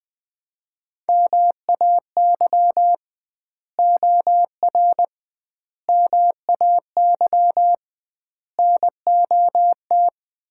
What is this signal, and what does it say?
Morse code, heard as a keyed tone, about 21 wpm, 700 Hz. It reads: MAY OR MAY NOT
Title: 2x Word-Spacing - Sets of 5 Words - 20wpm